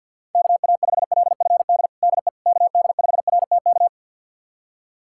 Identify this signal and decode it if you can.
Morse code, heard as a keyed tone, about 53 wpm, 700 Hz. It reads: KA5LFD DE KD5RTK
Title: USA Callsign DE Callsign - 6 Characters - 50wpm